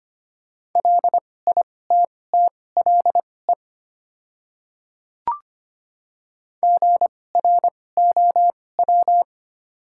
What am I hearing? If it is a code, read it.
Morse code, heard as a keyed tone, about 25 wpm, spaced as for 18 wpm, 690 Hz. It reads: LITTLE GROW
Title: 2x Character Spacing - Top 300 Words - 25wpm